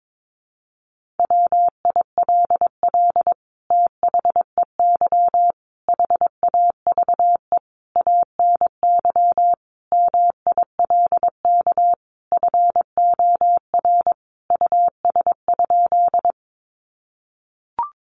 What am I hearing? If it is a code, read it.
Morse code, heard as a keyed tone, about 22 wpm, 700 Hz. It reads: WILL THEY HAVE ANY MILK FOR US?
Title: Sentences from Top 1500 Words 22wpm